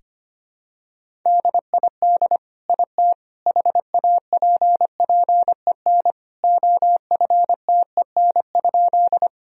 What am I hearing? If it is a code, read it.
Morse code, heard as a keyed tone, about 25 wpm, 700 Hz. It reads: DID IT HAPPEN OFTEN?
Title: Sentences from Top 1000 Words - Review Part 1 - 25wpm